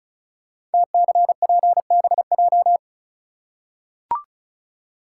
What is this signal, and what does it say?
Morse code, heard as a keyed tone, about 36 wpm, 690 Hz. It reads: TCPBJ
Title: Code Groups - 5 Letters - 35wpm